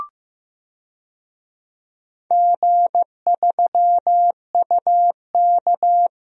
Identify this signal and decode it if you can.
Morse code, heard as a keyed tone, about 16 wpm, 700 Hz. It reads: G3UK
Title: DXCC-100 Call Signs - 4 Characters - Repeated 3x - 15wpm